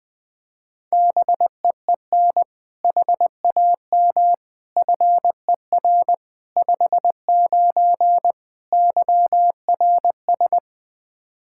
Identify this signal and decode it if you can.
Morse code, heard as a keyed tone, about 20 wpm, 700 Hz. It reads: BEEN HAM FER 59 YRS
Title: Practice QSO - Exchange 3 - More Info - 20wpm